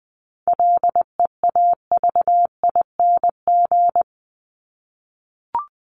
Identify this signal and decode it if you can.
Morse code, heard as a keyed tone, about 20 wpm, 700 Hz. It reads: LEAVING
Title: ING Suffix - Top 100 Words with Base Words 20wpm